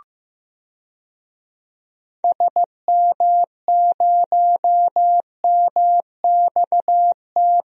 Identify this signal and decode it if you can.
Morse code, heard as a keyed tone, about 15 wpm, 700 Hz. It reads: SM0MXT
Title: DXCC-100 Call Signs - 6 Characters - Repeated 3x - 15wpm